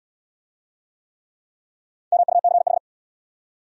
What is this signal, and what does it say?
Morse code, heard as a keyed tone, about 80 wpm, 700 Hz. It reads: MUCH